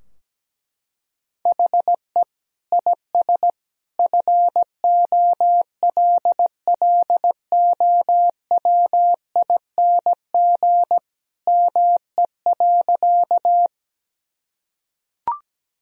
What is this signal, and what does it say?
Morse code, heard as a keyed tone, about 17 wpm, 700 Hz. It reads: HE IS FOLLOWING ME.
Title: Sentences from Top 1000 Words - Review Part 4 - 17wpm